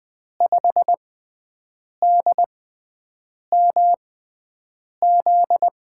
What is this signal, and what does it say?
Morse code, heard as a keyed tone, about 20 wpm, spaced as for 7 wpm, 700 Hz. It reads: 5DMZ
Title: USA Call Signs - 6 Characters - 6x Character Spacing - 20wpm